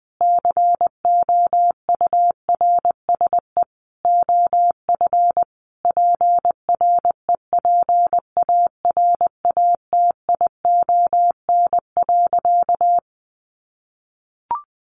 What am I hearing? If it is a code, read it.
Morse code, heard as a keyed tone, about 20 wpm, 700 Hz. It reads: COURSE OF PREPARATION.
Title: Sentences from Top 1400 Words 20wpm